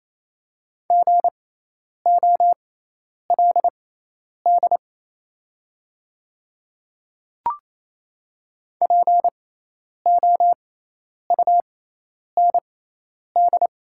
Morse code, heard as a keyed tone, about 28 wpm, spaced as for 10 wpm, 700 Hz. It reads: GOLD POUND
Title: 6x Character Spacing - Top 500 Words - 28wpm